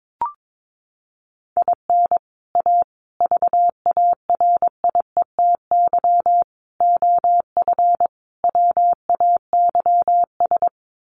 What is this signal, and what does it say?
Morse code, heard as a keyed tone, about 22 wpm, 700 Hz. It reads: IN A VARIETY OF WAYS
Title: Sets of 5 Words Encore 22wpm